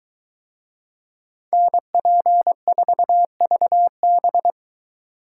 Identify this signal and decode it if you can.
Morse code, heard as a keyed tone, about 24 wpm, 700 Hz. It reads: NP4VB